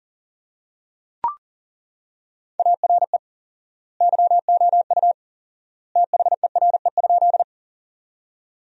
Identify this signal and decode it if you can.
Morse code, heard as a keyed tone, about 37 wpm, 700 Hz. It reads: ARE YOU THERE?
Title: 4x Word-Spacing - Sentences from Top 100 Words - 40wpm